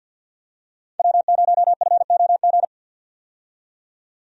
Morse code, heard as a keyed tone, about 50 wpm, 700 Hz. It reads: W0FOG
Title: USA Call Signs - 5 Characters - 50wpm